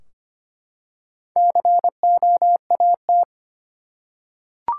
Morse code, heard as a keyed tone, about 25 wpm, 700 Hz. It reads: COAT
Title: Four Letter Words - 25wpm